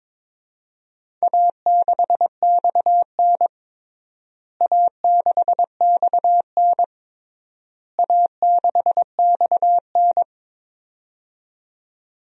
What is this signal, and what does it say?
Morse code, heard as a keyed tone, about 21 wpm, 700 Hz. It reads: A6XN A6XN A6XN
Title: DXCC-100 Call Signs - 4 Characters - Repeated 3x - 22wpm